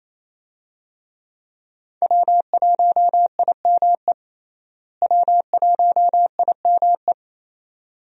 Morse code, heard as a keyed tone, about 28 wpm, 700 Hz. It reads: W1IME W1IME
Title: USA Call Signs - 5 Characters - Repeated 2x - 28wpm